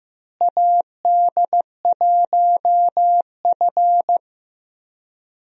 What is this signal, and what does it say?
Morse code, heard as a keyed tone, about 15 wpm, 700 Hz. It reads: AD1F
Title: USA Call Signs - 4 Characters - Repeated 3x - 15wpm